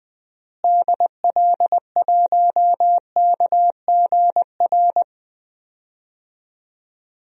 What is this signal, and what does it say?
Morse code, heard as a keyed tone, about 20 wpm, 700 Hz. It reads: DL1KGR
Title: DXCC-100 Call Signs - 6 Characters - Repeated 3x - 20wpm